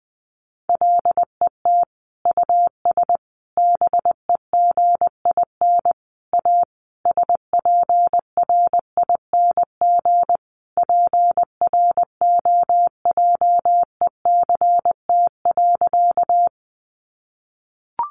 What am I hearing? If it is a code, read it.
Morse code, heard as a keyed tone, about 20 wpm, 700 Hz. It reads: LET US BEGIN A SPRING PROJECT.
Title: Sentences from Top 1400 Words 20wpm